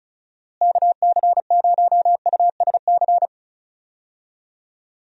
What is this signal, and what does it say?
Morse code, heard as a keyed tone, about 34 wpm, 700 Hz. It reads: KC0USC
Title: USA Call Signs - 6 Characters - Repeated 3x - 35wpm